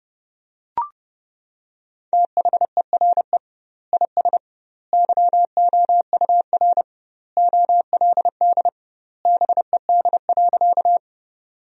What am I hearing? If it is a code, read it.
Morse code, heard as a keyed tone, about 29 wpm, 700 Hz. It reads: THERE IS YOUR OLD BED.
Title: 2x Word-Spacing - Sentences from Top 500 Words - 30wpm